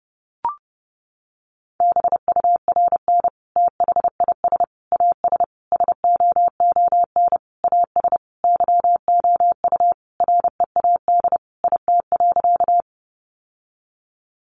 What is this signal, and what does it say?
Morse code, heard as a keyed tone, about 30 wpm, 700 Hz. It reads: BURN THIS AS SOON AS YOU READ IT.
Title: Sentences from Top 1300 Words 30wpm